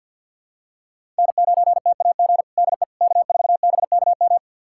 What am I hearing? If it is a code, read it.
Morse code, heard as a keyed tone, about 52 wpm, 700 Hz. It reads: N9TAG DE K4BKM
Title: USA Callsign DE Callsign - 5 Characters - 50wpm